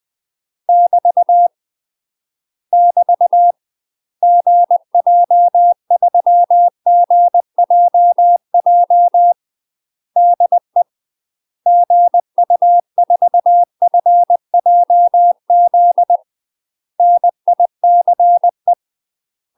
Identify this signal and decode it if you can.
Morse code, heard as a keyed tone, about 20 wpm, 700 Hz. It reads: <BT> <BT> GJ3GJJ DE GU4FJZ NICE
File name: UK MASTER QSOs TRAINING / 20WPM / UK MASTER QSOs 0026